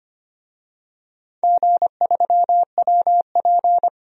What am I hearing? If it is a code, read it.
Morse code, heard as a keyed tone, about 25 wpm, 700 Hz. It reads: G3WP
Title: DXCC-100 Call Signs - 4 Characters - 25wpm